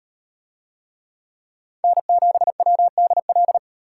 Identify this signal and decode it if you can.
Morse code, heard as a keyed tone, about 37 wpm, 700 Hz. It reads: N7WDL